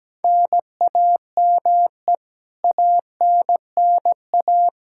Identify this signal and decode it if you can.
Morse code, heard as a keyed tone, about 17 wpm, 700 Hz. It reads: NAME ANNA
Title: K1USN SST QSO Practice 17wpm